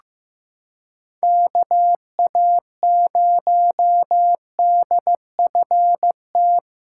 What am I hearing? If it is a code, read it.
Morse code, heard as a keyed tone, about 15 wpm, 700 Hz. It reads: KA0DFT
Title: USA Callsign DE Callsign - 6 Characters - 15wpm